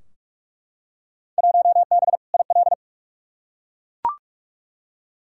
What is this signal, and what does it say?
Morse code, heard as a keyed tone, about 42 wpm, 700 Hz. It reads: 1B IL
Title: Field Day Exchanges - 45wpm